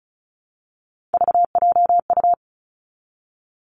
Morse code, heard as a keyed tone, about 36 wpm, 700 Hz. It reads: VJU
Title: Three Challenging Letter-Number Combinations 35wpm